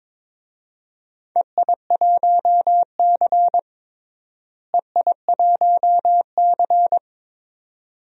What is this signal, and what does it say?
Morse code, heard as a keyed tone, about 21 wpm, 700 Hz. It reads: EI1C EI1C
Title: DXCC-100 Call Signs - 4 Characters - Repeated 2x - 22wpm